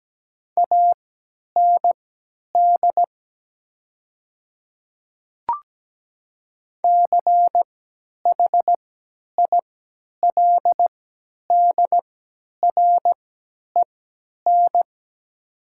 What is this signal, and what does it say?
Morse code, heard as a keyed tone, about 17 wpm, spaced as for 10 wpm, 700 Hz. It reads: AND CHILDREN
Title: 3x Character Spacing - Top 300 Words - 17wpm